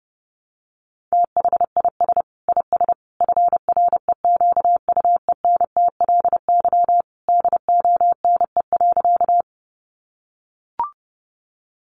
Morse code, heard as a keyed tone, about 30 wpm, 700 Hz. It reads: THIS IS FREQUENTLY DONE.
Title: Sentences from Top 1300 Words 30wpm